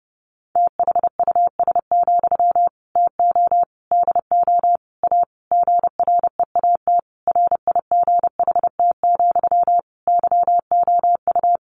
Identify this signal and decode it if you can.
Morse code, heard as a keyed tone, about 28 wpm, 700 Hz. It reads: THUS, TO DO A GREAT RIGHT, YOU
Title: Sentences from Top 900 Words 30wpm